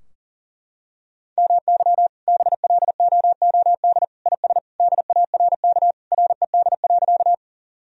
Morse code, heard as a keyed tone, about 40 wpm, 700 Hz. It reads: MY BLOOD IS DARK RED.
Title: Sentences from Top 1000 Words - Review Part 2 - 40wpm